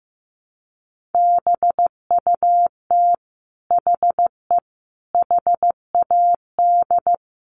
Morse code, heard as a keyed tone, about 15 wpm, 700 Hz. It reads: BUT HE HAD
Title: Sentences from Top 1100 Words 15wpm